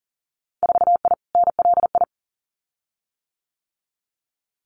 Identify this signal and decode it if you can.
Morse code, heard as a keyed tone, about 42 wpm, 700 Hz. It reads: 4I NLI
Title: Winter Field Day Exchanges Encore 40wpm